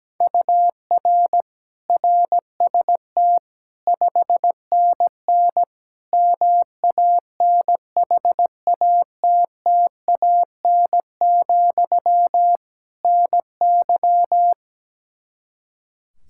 Morse code, heard as a keyed tone, about 17 wpm, 700 Hz. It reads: UR RST 5NN MANHATTAN, NY
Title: Practice QSO - Exchange 1 - Beginning - 17wpm